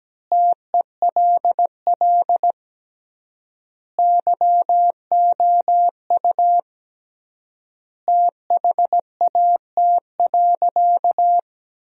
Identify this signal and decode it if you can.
Morse code, heard as a keyed tone, about 17 wpm, 700 Hz. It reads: TELL YOU THAT.
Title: 3x Word-Spacing - Sentences from Top 300 Words - 17wpm